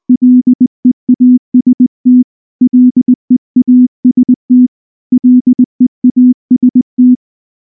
Morse code, heard as a keyed tone, about 19 wpm, 250 Hz. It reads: LEAST LEAST LEAST
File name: morse / morse-training-20WPM-1